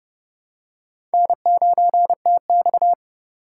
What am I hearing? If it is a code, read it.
Morse code, heard as a keyed tone, about 32 wpm, 700 Hz. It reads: N9TX
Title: USA Call Signs - 4x Character Spacing - 30wpm